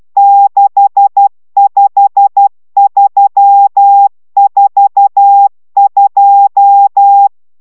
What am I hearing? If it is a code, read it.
Morse code, heard as a keyed tone, about 12 wpm, 800 Hz. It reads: 65342